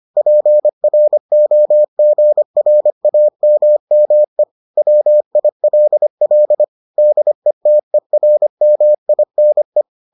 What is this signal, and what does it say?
Morse code, heard as a keyed tone, about 25 wpm, 600 Hz. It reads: PROGRAMME WILL DETERMINE